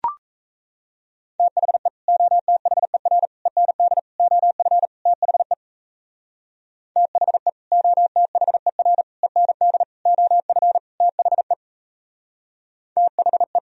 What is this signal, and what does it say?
Morse code, heard as a keyed tone, about 34 wpm, 700 Hz. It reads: THE OTHER END OF THE THE OTHER END OF THE THE
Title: Speed-Racing - Sets of 5 Words 28wpm